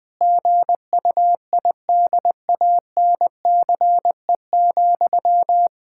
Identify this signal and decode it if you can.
Morse code, heard as a keyed tone, about 20 wpm, 700 Hz. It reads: GUIDANCE,
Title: Sentences from Top 1600 Words 20wpm